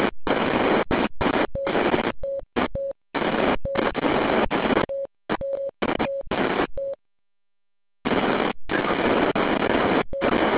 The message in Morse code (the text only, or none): none